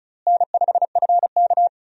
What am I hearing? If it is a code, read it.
Morse code, heard as a keyed tone, about 36 wpm, 700 Hz. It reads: N5FK
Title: USA Call Signs - 4 Characters - Repeated 2x - 35wpm